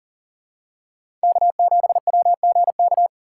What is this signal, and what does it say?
Morse code, heard as a keyed tone, about 40 wpm, 700 Hz. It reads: K7WGK
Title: USA Call Signs - 5 Characters - 2x Character Spacing - 40wpm